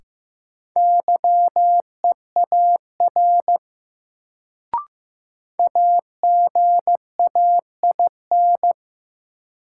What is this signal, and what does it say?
Morse code, heard as a keyed tone, about 15 wpm, 700 Hz. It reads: YEAR AGAIN
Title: Sentences from Top 200 Words - New Word Drill - 15wpm